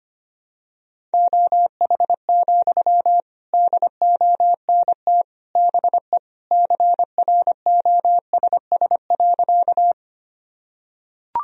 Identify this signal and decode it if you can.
Morse code, heard as a keyed tone, about 25 wpm, 700 Hz. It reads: OH, DONT BE CROSS.
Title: Sentences from Top 1000 Words - Review Part 1 - 25wpm